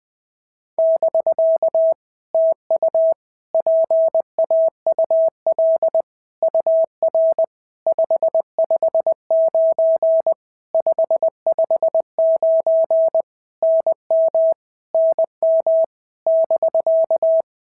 Morse code, heard as a keyed tone, about 20 wpm, 650 Hz. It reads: <BK> TU PAUL UR 559 559 NM NM <BK>
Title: POTA QSO Practice - 20wpm